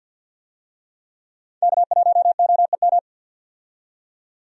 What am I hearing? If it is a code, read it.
Morse code, heard as a keyed tone, about 46 wpm, 700 Hz. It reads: K1OEM
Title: USA Call Signs - 5 Characters - 50wpm